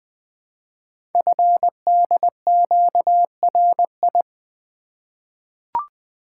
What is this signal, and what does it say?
Morse code, heard as a keyed tone, about 20 wpm, 700 Hz. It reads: FDQRI
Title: Code Groups - 5 Letters - 20wpm